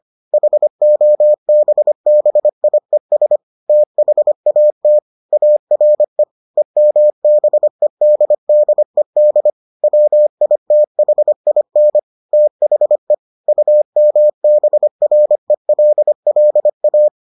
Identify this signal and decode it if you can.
Morse code, heard as a keyed tone, about 25 wpm, 600 Hz. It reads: HOBBIES THAT ARE EMBEDDED WITHIN THE UMBRELLA